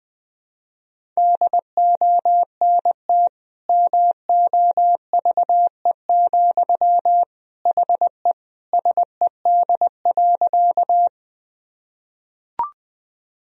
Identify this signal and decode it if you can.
Morse code, heard as a keyed tone, about 20 wpm, 700 Hz. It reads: DONT MOVE, HE SED.
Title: Sentences from Top 300 Words and Abbreviations 20wpm